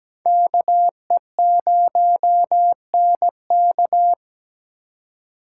Morse code, heard as a keyed tone, about 20 wpm, 700 Hz. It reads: KE0NK
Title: Speed Racing - USA 2x2 Call Signs 17wpm